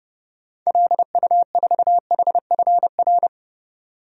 Morse code, heard as a keyed tone, about 30 wpm, 700 Hz. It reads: LU4HFR
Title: DXCC-100 Call Signs - 6 Characters - 30wpm